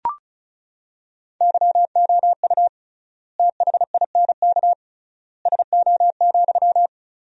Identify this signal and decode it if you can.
Morse code, heard as a keyed tone, about 34 wpm, 700 Hz. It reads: YOU THINK SO,
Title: 3x Word-Spacing - Sentences from Top 200 Words - 35wpm